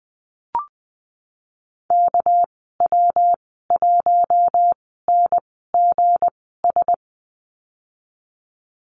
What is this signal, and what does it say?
Morse code, heard as a keyed tone, about 20 wpm, spaced as for 14 wpm, 700 Hz. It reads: KW1NGS
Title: 238 Lesson 38g - Letter q - Call Signs 20wpm